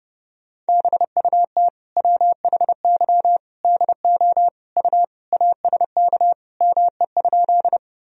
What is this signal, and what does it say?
Morse code, heard as a keyed tone, about 32 wpm, 700 Hz. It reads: BUT WHY DO U ASK ME?
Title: Sentences from Top 200 Words and Abbreviations 30wpm